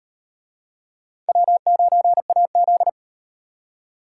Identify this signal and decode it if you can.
Morse code, heard as a keyed tone, about 38 wpm, 700 Hz. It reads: W9AZ